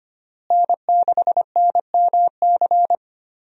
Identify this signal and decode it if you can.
Morse code, heard as a keyed tone, about 25 wpm, 700 Hz. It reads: N6NMC
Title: USA Call Signs - 5 Characters - Repeated 2x - 25wpm